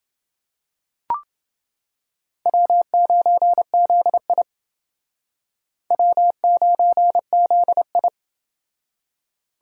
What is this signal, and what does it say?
Morse code, heard as a keyed tone, about 28 wpm, 700 Hz. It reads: W9ZI W9ZI